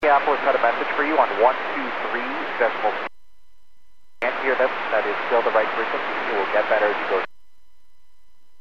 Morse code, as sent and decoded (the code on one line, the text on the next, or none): none